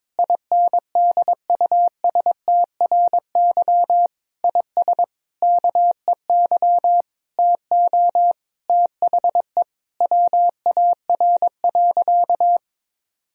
Morse code, heard as a keyed tone, about 22 wpm, 700 Hz. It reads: INDUSTRY IS KEY TO THE WAR.
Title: Sentences from Top 1000 Words - Review Part 4 - 22wpm